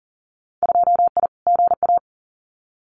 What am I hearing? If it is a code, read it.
Morse code, heard as a keyed tone, about 38 wpm, 700 Hz. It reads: 2I GA
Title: Winter Field Day Exchanges 40wpm